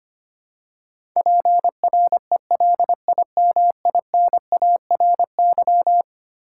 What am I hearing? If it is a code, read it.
Morse code, heard as a keyed tone, about 25 wpm, 700 Hz. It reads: PRELIMINARY
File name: practice / 25wpm / Eleven Letter Words - 25wpm